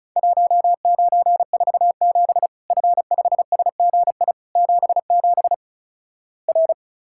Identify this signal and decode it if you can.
Morse code, heard as a keyed tone, about 36 wpm, 700 Hz. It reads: 1947 F5SGI 77 R
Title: Worked All Europe DX Contest - QTC Exchange - 35wpm